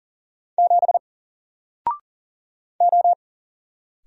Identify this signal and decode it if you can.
Morse code, heard as a keyed tone, about 40 wpm, 700 Hz. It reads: 7 O
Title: Single Character (Letter:Number:Punctuation:Prosign) Alt 40wpm